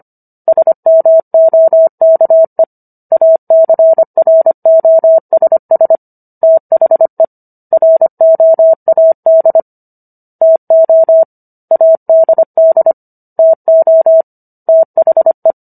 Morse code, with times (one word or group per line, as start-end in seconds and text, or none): 0.5-2.6 SMOKE
3.1-6.0 ACROSS
6.4-7.2 THE
7.7-9.6 ROAD
10.4-11.2 TO
11.7-12.9 ADD
13.4-14.2 TO
14.7-15.5 THE